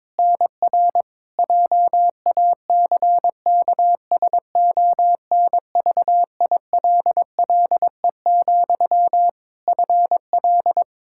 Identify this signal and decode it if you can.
Morse code, heard as a keyed tone, about 22 wpm, 700 Hz. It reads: NR JACKSONVILLE, FL
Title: Practice QSO - Exchange 1 - Beginning - 22wpm